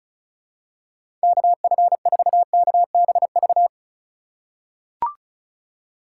Morse code, heard as a keyed tone, about 36 wpm, 700 Hz. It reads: KF4KBV
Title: USA Call Signs - 6 Characters - 4x Character Spacing - 35wpm